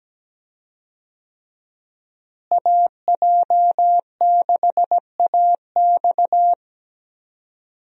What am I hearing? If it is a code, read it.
Morse code, heard as a keyed tone, about 17 wpm, 700 Hz. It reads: AJ6AX
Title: USA Call Signs - 5 Characters - Repeated 3x - 17wpm